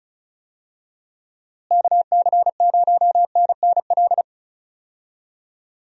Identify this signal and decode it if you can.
Morse code, heard as a keyed tone, about 36 wpm, 700 Hz. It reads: KC0NNL
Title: USA Call Signs - 6 Characters - Repeated 3x - 35wpm